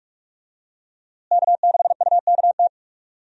Morse code, heard as a keyed tone, about 41 wpm, 700 Hz. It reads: K6UKT
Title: USA Call Signs - 5 Characters - Repeated 3x - 45wpm